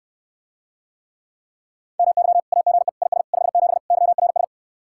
Morse code, heard as a keyed tone, about 69 wpm, 700 Hz. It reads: MY AGE IS 57 YRS